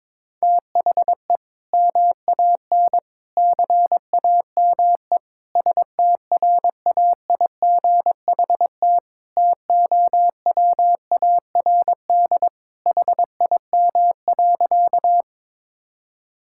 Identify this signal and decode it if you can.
Morse code, heard as a keyed tone, about 22 wpm, 700 Hz. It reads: THE MAN CAME STRAIGHT TOWARD HIM.
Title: Sentences from Top 1000 Words - Review Part 6 - 22wpm